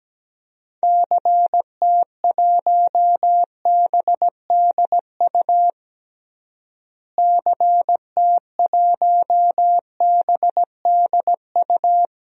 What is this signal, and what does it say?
Morse code, heard as a keyed tone, about 17 wpm, 700 Hz. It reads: CT1BDU CT1BDU
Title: DXCC-100 Call Signs - 6 Characters - Repeated 2x - 17wpm